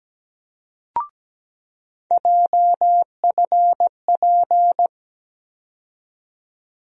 Morse code, heard as a keyed tone, about 17 wpm, 700 Hz. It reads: JFP
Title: USA Call Sign Postfixes - 3 Letters 17wpm